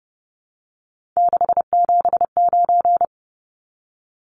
Morse code, heard as a keyed tone, about 30 wpm, 700 Hz. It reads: 679
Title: Numbers - 3-Digit 30wpm